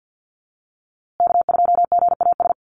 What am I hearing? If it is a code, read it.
Morse code, heard as a keyed tone, about 50 wpm, 700 Hz. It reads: K2GAS
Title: USA Call Signs Repeated 2x 50wpm